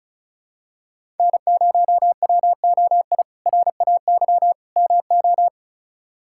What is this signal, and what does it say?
Morse code, heard as a keyed tone, about 34 wpm, 700 Hz. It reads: N0WOI RAY MO
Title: K1USN SST QSO Practice - 35wpm